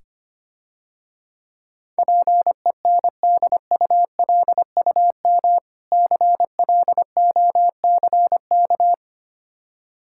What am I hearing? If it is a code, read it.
Morse code, heard as a keyed tone, about 25 wpm, 700 Hz. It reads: PENDULUM CLOCK
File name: practice / 25wpm / Difficult Words - Speed-Racing with Hints - 25wpm